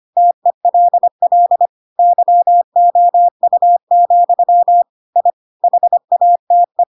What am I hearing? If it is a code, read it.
Morse code, heard as a keyed tone, about 25 wpm, 700 Hz. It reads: TELL YOU, I HATE
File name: Pinocchio / Pinocchio ch4 25 25 ik1hge